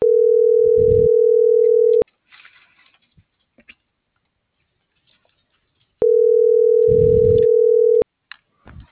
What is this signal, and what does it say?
Ringback tone (North America), 440 + 480 Hz.